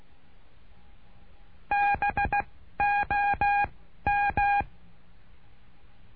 Morse code, heard as a keyed tone, about 16 wpm, spaced as for 12 wpm, 810 Hz. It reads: BOM
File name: cursos / CURSO DE TELEGRAFIA / Telegrafia - Trilha 09